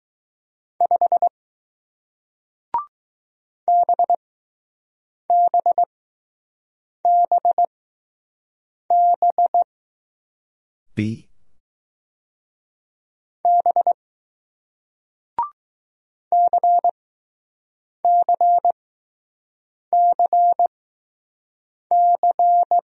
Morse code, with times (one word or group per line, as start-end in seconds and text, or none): 0.8-1.3 5
3.7-9.6 BBBB
13.5-13.9 B
16.3-22.8 CCCC